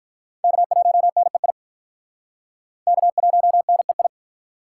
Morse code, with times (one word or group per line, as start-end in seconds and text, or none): 0.4-1.5 K1NEI
2.9-4.1 K1NEI